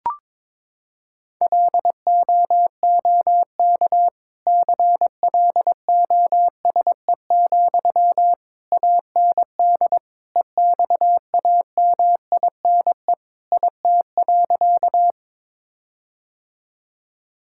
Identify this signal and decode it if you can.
Morse code, heard as a keyed tone, about 22 wpm, 700 Hz. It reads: LOOK CLOSE, AND EXAMINE IT.
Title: Sentences from Top 1600 Words 22wpm